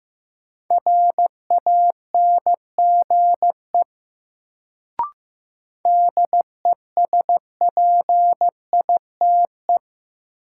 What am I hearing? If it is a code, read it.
Morse code, heard as a keyed tone, about 15 wpm, 700 Hz. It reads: RANGE DESPITE
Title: Sets of 4 Words - New Word Drill - 15wpm